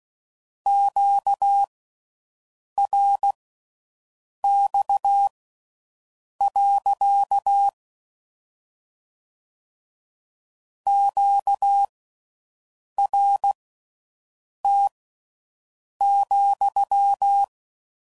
Morse code, heard as a keyed tone, about 16 wpm, spaced as for 6 wpm, 800 Hz. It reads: QRX. QRT,